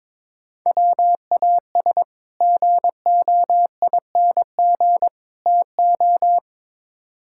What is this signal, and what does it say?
Morse code, heard as a keyed tone, about 22 wpm, 700 Hz. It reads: WAS GOING TO
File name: practice / 22wpm / Build A Phrase - 5-Word Phrases - 22wpm